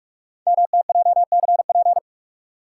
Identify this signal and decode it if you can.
Morse code, heard as a keyed tone, about 44 wpm, 690 Hz. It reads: MTJCP
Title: Code Groups - 5 Letters - 45wpm